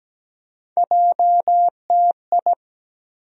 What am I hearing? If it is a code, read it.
Morse code, heard as a keyed tone, about 17 wpm, 700 Hz. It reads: JTI
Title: USA Call Sign Postfixes - 3 Letters 17wpm